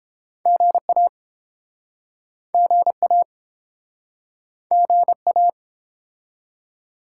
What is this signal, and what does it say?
Morse code, heard as a keyed tone, about 29 wpm, 700 Hz. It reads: GA GA GA